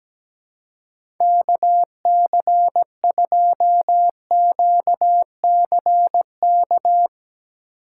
Morse code, heard as a keyed tone, about 17 wpm, 700 Hz. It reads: KC2QCK
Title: USA Call Signs - 2x Character Spacing - 17wpm